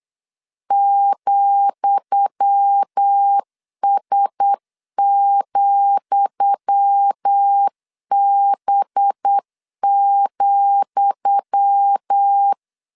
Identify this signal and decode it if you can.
Morse code, heard as a keyed tone, about 8 wpm, 800 Hz. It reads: ,S,B,
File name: LEZIONI / 10 SIGNS / 10D COMA 8wpm 3749